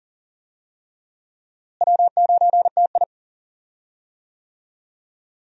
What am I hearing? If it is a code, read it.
Morse code, heard as a keyed tone, about 40 wpm, 700 Hz. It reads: W9TI
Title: USA Call Signs - 4 Characters - Repeated 3x - 40wpm